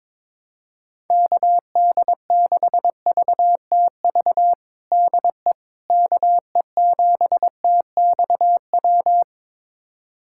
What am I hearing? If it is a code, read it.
Morse code, heard as a keyed tone, about 22 wpm, 700 Hz. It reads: KD6VTV DE KE7TXW
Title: USA Callsign DE Callsign 22wpm